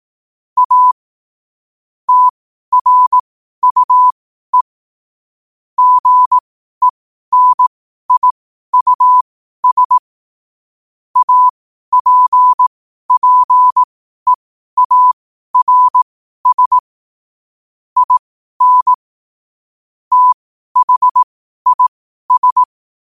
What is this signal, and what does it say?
Morse code, heard as a keyed tone, about 18 wpm, spaced as for 12 wpm, 1000 Hz. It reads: A TRUE GENIUS APPEARS IN THIS
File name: qotdfiles / 12wpm